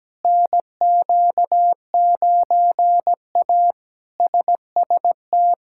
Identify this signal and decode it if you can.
Morse code, heard as a keyed tone, about 17 wpm, 700 Hz. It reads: NQ9A SST
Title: K1USN SST QSO Practice - 17wpm